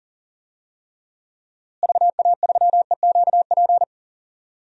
Morse code, heard as a keyed tone, about 40 wpm, 700 Hz. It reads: VA3EQP
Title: DXCC-100 Call Signs - 6 Characters - Repeated 3x - 40wpm